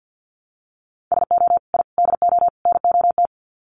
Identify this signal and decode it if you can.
Morse code, heard as a keyed tone, about 50 wpm, 700 Hz. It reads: SO I DO NOT
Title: Sets of 4 Words 50wpm